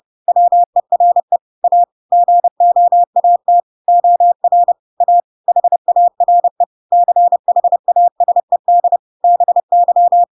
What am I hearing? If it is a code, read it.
Morse code, heard as a keyed tone, about 30 wpm, 700 Hz. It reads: WERE A GOAT OR A HARE CHASED BY